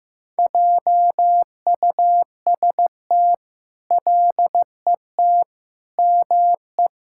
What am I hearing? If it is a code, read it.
Morse code, heard as a keyed tone, about 15 wpm, 700 Hz. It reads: JUST LET ME
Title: Sentences from Top 1000 Words - Review Part 5 - 15wpm